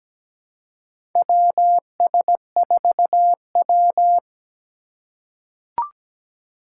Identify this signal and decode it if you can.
Morse code, heard as a keyed tone, about 17 wpm, 700 Hz. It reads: WS4W
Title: USA Call Signs - 4 Characters - 6x Character Spacing - 17wpm